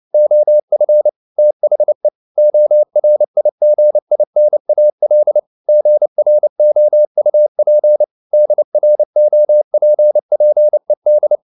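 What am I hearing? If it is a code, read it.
Morse code, heard as a keyed tone, about 29 wpm, 600 Hz. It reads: OF THE ORIGINAL GROUP DROPPED
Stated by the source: Morse code at 29 wpm, sending Ken Follett's novel Never in English, from CW med Gnister.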